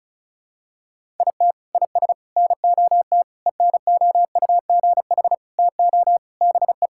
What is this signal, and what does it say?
Morse code, heard as a keyed tone, about 36 wpm, 700 Hz. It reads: IT IS NOT ENOUGH TO BE
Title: Sentences from Top 1000 Words - Review Part 6 - 35wpm